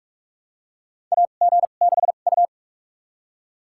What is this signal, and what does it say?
Morse code, heard as a keyed tone, about 45 wpm, spaced as for 32 wpm, 700 Hz. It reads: AG6U
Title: USA Call Signs - 4 Characters - 2x Character Spacing - 45wpm